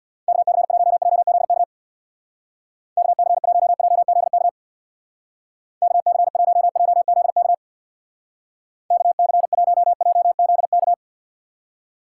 Morse code, heard as a keyed tone, about 65 wpm, 700 Hz. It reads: KC1JZK KC1JZK KC1JZK KC1JZK